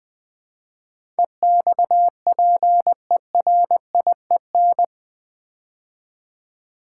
Morse code, heard as a keyed tone, about 20 wpm, 700 Hz. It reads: EXPERIEN
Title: Build A Word - 4 Letters At A Time - 11-Letter Words - 20wpm